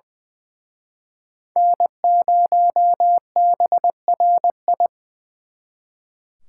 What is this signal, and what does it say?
Morse code, heard as a keyed tone, about 25 wpm, 700 Hz. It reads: N0BRI
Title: Speed Racing - USA Call Signs 20wpm